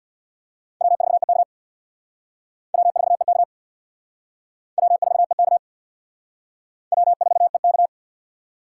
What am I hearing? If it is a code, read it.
Morse code, heard as a keyed tone, about 62 wpm, 700 Hz. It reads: W4EK W4EK W4EK W4EK
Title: Speed Racing - USA 1x2 Call Signs 50wpm